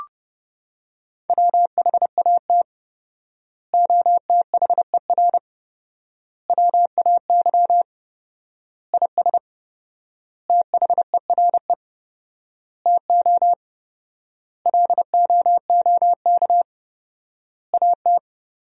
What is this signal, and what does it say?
Morse code, heard as a keyed tone, about 30 wpm, 700 Hz. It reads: WHAT OTHER WAY IS THERE TO LOOK AT
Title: 4x Word-Spacing - Sentences from Top 100 Words - 30wpm